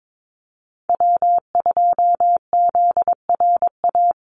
Morse code, heard as a keyed tone, about 22 wpm, 700 Hz. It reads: W2ZRA
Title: Call Signs Repeated 2x - 22wpm